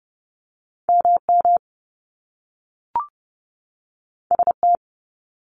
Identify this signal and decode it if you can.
Morse code, heard as a keyed tone, about 30 wpm, 700 Hz. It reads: MM ST
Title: Top Two Letter Combinations in Words 30wpm